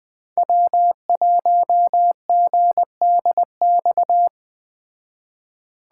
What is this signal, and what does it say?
Morse code, heard as a keyed tone, about 20 wpm, 700 Hz. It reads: W1GDX